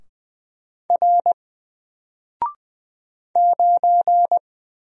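Morse code, heard as a keyed tone, about 20 wpm, 700 Hz. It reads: R 9
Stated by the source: Single letter and number Morse drill at 20 wpm, from N4LCD.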